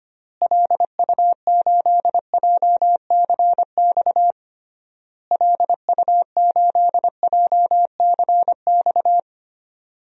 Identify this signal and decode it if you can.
Morse code, heard as a keyed tone, about 25 wpm, 700 Hz. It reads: LU8JCX LU8JCX
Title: DXCC-100 Call Signs - 6 Characters - Repeated 3x - 25wpm